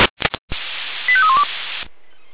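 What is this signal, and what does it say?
Five-tone selective call (ZVEI), heard as quick runs of five tones, tones of about 70 ms. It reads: 85312
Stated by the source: ZVEI five-tone call received on an SDR.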